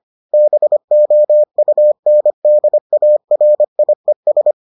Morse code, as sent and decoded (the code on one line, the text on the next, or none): -... --- ..- -. -.. .- .-. .. ...
BOUNDARIES